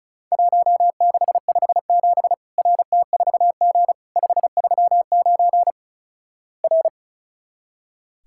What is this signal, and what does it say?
Morse code, heard as a keyed tone, about 36 wpm, 700 Hz. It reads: 1657 RT4G 539 R
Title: Worked All Europe DX Contest - QTC Exchange - 35wpm